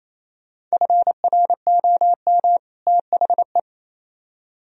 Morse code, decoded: FROM THE

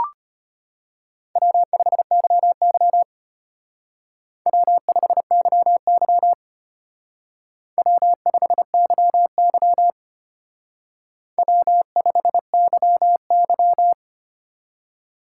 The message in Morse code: W5YY W5YY W5YY W5YY

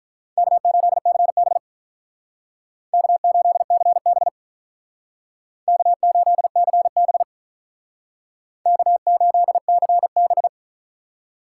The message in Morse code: K8CB K8CB K8CB K8CB

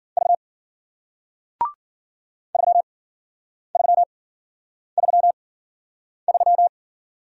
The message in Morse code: V 3333

